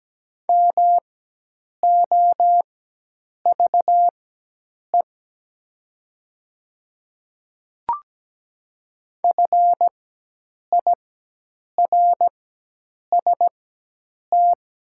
Morse code, heard as a keyed tone, about 17 wpm, spaced as for 8 wpm, 700 Hz. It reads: MOVE FIRST
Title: 4x Character Spacing - Top 200 Words - 17wpm